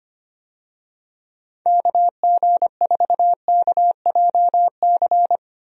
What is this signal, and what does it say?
Morse code, heard as a keyed tone, about 25 wpm, 700 Hz. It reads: KG4KJC
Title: USA Call Signs - 6 Characters - Repeated 2x - 25wpm